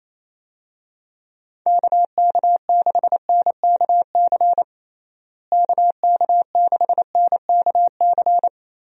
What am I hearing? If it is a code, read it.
Morse code, heard as a keyed tone, about 28 wpm, 700 Hz. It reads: KK6NKC KK6NKC
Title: USA Call Signs - 6 Characters - Repeated 3x - 28wpm